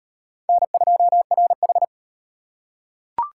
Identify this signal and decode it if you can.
Morse code, heard as a keyed tone, about 38 wpm, 700 Hz. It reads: N2RH